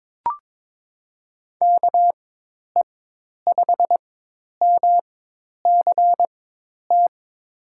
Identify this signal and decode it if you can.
Morse code, heard as a keyed tone, about 22 wpm, spaced as for 10 wpm, 700 Hz. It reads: KE5MCT
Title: USA Call Signs - 6 Characters - 4x Character Spacing - 22wpm